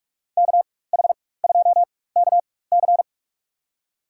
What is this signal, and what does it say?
Morse code, heard as a keyed tone, about 46 wpm, spaced as for 21 wpm, 700 Hz. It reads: KH2KC
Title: USA Call Signs - 5 Characters - 4x Character Spacing - 45wpm